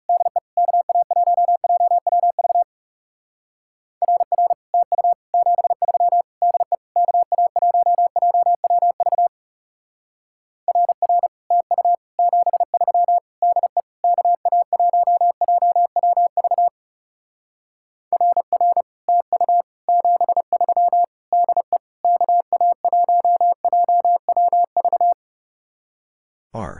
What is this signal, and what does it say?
Morse code, heard as a keyed tone, about 34 wpm, 700 Hz. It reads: DE KA1JWV RR TU 73 DE KA1JWV RR TU 73 DE KA1JWV RR TU 73 DE KA1JWV
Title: Speed-Racing Common QSO Phrases - 30wpm